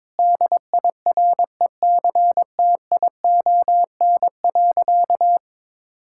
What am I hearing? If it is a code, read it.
Morse code, heard as a keyed tone, about 22 wpm, 700 Hz. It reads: DIRECTION.
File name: Sentences with 7-Word Phrases - 22wpm